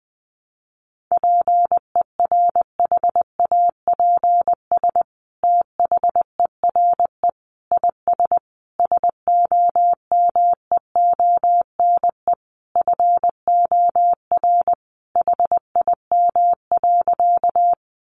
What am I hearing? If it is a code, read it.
Morse code, heard as a keyed tone, about 20 wpm, 700 Hz. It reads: PERHAPS THERE IS SOMEONE FOR HIM.
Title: Sentences from Top 600 Words 20wpm